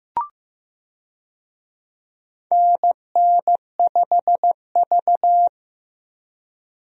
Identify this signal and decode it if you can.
Morse code, heard as a keyed tone, about 16 wpm, 700 Hz. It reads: NN5V
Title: USA Call Signs - 4 Characters - Repeated 3x - 15wpm